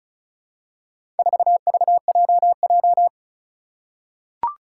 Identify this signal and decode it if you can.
Morse code, heard as a keyed tone, about 36 wpm, 700 Hz. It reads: 4VJJ
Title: Four Challenging Letter-Number Combinations 35wpm